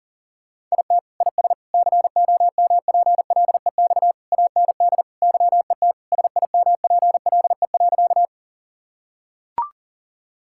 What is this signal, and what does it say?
Morse code, heard as a keyed tone, about 40 wpm, 700 Hz. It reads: IT IS COMPLEX AND YET SIMPLE.
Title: Sentences from Top 1000 Words - Review Part 6 - 40wpm